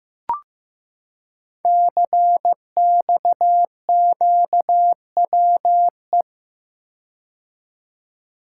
Morse code, heard as a keyed tone, about 15 wpm, 700 Hz. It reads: CXQWE
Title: Code Groups - 5 Letters - 15wpm